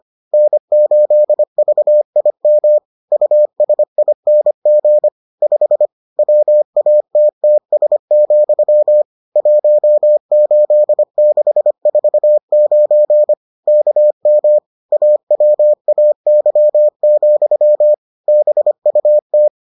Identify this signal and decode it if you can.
Morse code, heard as a keyed tone, about 25 wpm, 600 Hz. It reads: N8VIM USING 5 WATTS, 18649 KM AWAY, BUT